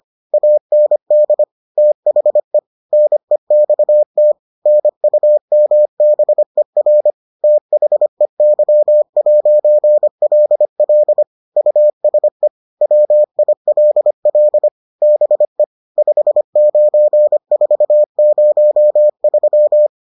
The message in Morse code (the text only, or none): AND THE NEXT NUMBER THEY'LL USE WILL BE 59403